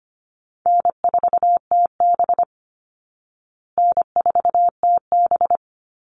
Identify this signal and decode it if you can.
Morse code, heard as a keyed tone, about 25 wpm, 700 Hz. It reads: N4TB N4TB
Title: Easy Call Signs Repeated 3x - 25wpm